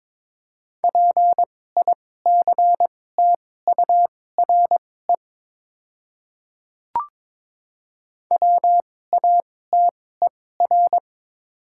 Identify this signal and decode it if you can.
Morse code, heard as a keyed tone, about 22 wpm, spaced as for 16 wpm, 700 Hz. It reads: PICTURE WATER
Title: 2x Character Spacing - Top 500 Words - 22wpm